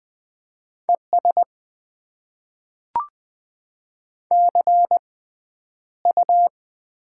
Morse code, heard as a keyed tone, about 20 wpm, spaced as for 10 wpm, 700 Hz. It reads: ES CU
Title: Top 20 QSO Elements - 6x Character Spacing - 20wpm